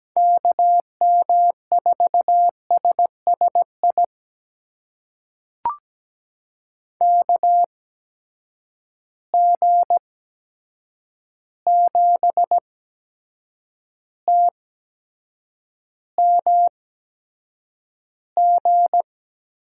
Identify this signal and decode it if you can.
Morse code, heard as a keyed tone, about 17 wpm, spaced as for 7 wpm, 700 Hz. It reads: KM4SSI KG7TMG
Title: USA Call Signs - 6 Characters - 8x Character Spacing - 17wpm